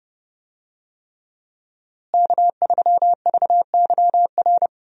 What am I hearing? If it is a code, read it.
Morse code, heard as a keyed tone, about 30 wpm, 700 Hz. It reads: K3VYR
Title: USA Call Signs - 5 Characters - Repeated 3x - 30wpm